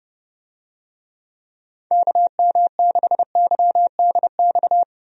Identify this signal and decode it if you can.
Morse code, heard as a keyed tone, about 32 wpm, 700 Hz. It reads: KM6YDX
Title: USA Call Signs - 6 Characters - Repeated 3x - 30wpm